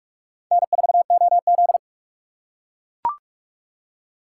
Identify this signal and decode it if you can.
Morse code, heard as a keyed tone, about 44 wpm, 700 Hz. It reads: N4OZ